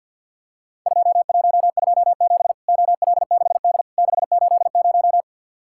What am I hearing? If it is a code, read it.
Morse code, heard as a keyed tone, about 50 wpm, 700 Hz. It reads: 2127 OL6D 680
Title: Worked All Europe DX Contest - QTC Exchange - 50wpm